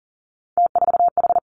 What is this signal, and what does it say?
Morse code, heard as a keyed tone, about 41 wpm, 700 Hz. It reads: T4H
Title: Three Challenging Letter-Number Combinations 40wpm